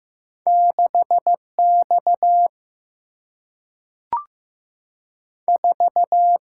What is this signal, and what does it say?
Morse code, heard as a keyed tone, about 15 wpm, 700 Hz. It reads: 6X 4
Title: Single-Letter Single-Number 15wpm